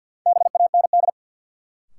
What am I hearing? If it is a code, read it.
Morse code, heard as a keyed tone, about 53 wpm, 700 Hz. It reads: BAND
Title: Sentences from Top 1600 Words - New Word Drill - 50wpm